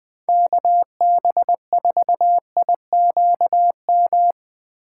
Morse code, heard as a keyed tone, about 22 wpm, 700 Hz. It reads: KB4IQM
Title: Speed Racing - USA 2x3 Call Signs 20wpm